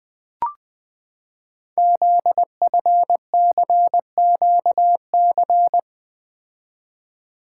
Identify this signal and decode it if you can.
Morse code, heard as a keyed tone, about 20 wpm, 700 Hz. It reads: ZFCQC
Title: Code Groups - 5 Letters - 20wpm